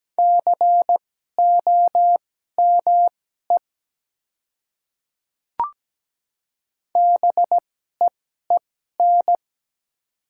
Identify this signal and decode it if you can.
Morse code, heard as a keyed tone, about 17 wpm, spaced as for 12 wpm, 700 Hz. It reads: COME BEEN
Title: 2x Character Spacing - Top 100 Words - 17wpm